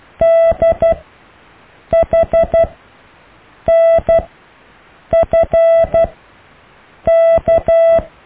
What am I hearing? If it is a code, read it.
Morse code, heard as a keyed tone, about 12 wpm, spaced as for 6 wpm, 660 Hz. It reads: DHNFK
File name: cw / EKTPV+BFHNW